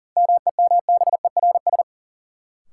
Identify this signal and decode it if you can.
Morse code, heard as a keyed tone, about 40 wpm, 700 Hz. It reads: MEMBERS
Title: Sets of 3 Words - New Word Drill - 40wpm